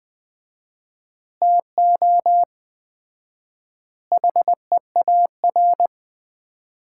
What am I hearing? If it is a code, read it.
Morse code, heard as a keyed tone, about 20 wpm, 700 Hz. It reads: TO HEAR
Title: Common QSO Phrases - 4x Word-Spacing - 20wpm